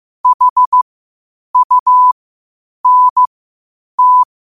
Morse code, heard as a keyed tone, about 15 wpm, spaced as for 8 wpm, 1000 Hz. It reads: HUNT
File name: qotdfiles / 8wpm